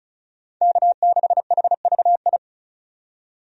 Morse code, heard as a keyed tone, about 36 wpm, 700 Hz. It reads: K6HVI
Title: USA Call Signs - 5 Characters - 35wpm